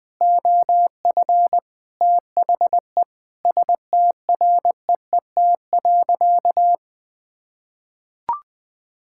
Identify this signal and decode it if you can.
Morse code, heard as a keyed tone, about 20 wpm, 700 Hz. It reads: OF THE STREET.
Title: Sentences with 6-Word Phrases - 20wpm